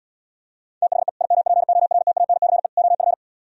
Morse code, heard as a keyed tone, about 76 wpm, 700 Hz. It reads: THE IMPORTANCE OF